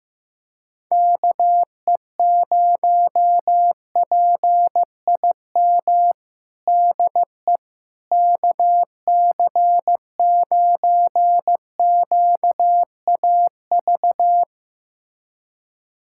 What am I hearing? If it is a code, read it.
Morse code, heard as a keyed tone, about 16 wpm, 700 Hz. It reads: KE0PIM DE KC9QAV
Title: USA Callsign DE Callsign - 6 Characters - 15wpm